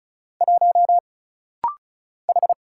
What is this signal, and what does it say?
Morse code, heard as a keyed tone, about 34 wpm, 700 Hz. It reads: 1 H
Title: Single Letter-Number Alt - Rapid-Fire 35wpm